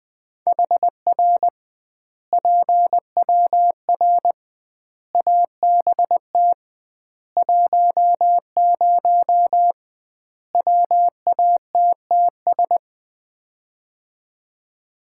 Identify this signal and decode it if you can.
Morse code, heard as a keyed tone, about 20 wpm, 700 Hz. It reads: HR PWR ABT 10 WATTS
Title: Common QSO Phrases - 2x Word-Spacing - 20wpm